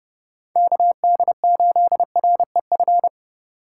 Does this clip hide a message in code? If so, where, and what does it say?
Yes: Morse code from 0.6 to 3.1 s, heard as a keyed tone, about 31 wpm, 700 Hz. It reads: KD8REF